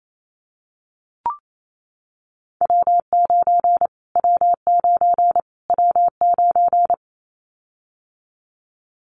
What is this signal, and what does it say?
Morse code, heard as a keyed tone, about 28 wpm, 700 Hz. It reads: W9 W9 W9
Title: Callsign Prefixes for Top 25 DXCC - Repeated 3x - 28wpm